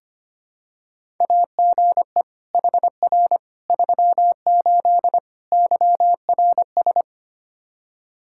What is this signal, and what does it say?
Morse code, heard as a keyed tone, about 25 wpm, 700 Hz. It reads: AGE HR 38 YRS
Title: Practice QSO - Exchange 3 - More Info - 25wpm